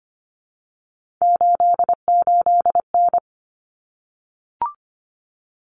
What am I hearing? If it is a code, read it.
Morse code, heard as a keyed tone, about 25 wpm, 700 Hz. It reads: 88N
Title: Common Cut Numbers - 3-Digit 25wpm